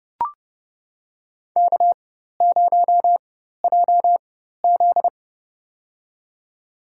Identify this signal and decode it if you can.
Morse code, heard as a keyed tone, about 30 wpm, spaced as for 14 wpm, 700 Hz. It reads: K0JZ
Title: USA Call Signs - 4 Characters - 4x Character Spacing - 30wpm